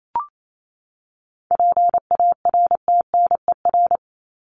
Morse code, heard as a keyed tone, about 28 wpm, 700 Hz. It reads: PARTNER
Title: Top 1501-1750 Words 28wpm